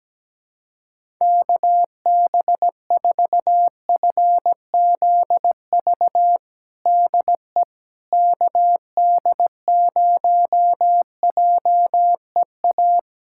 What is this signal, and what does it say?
Morse code, heard as a keyed tone, about 17 wpm, 700 Hz. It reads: KB4FZV DE KD0JEA
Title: USA Callsign DE Callsign - 6 Characters - 17wpm